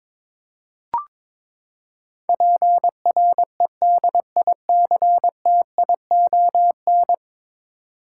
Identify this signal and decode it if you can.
Morse code, heard as a keyed tone, about 22 wpm, 700 Hz. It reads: PREDICTION
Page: morse-code-ninja-practice-22wpm